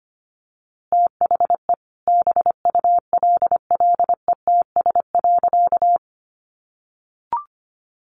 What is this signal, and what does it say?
Morse code, heard as a keyed tone, about 25 wpm, 700 Hz. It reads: THE BULLETS.
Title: A Sound Of Thunder 25wpm